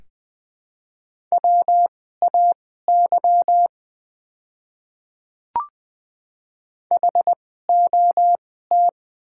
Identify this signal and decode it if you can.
Morse code, heard as a keyed tone, about 20 wpm, spaced as for 15 wpm, 700 Hz. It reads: WAY HOT
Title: 2x Character Spacing - Top 100 Words - 20wpm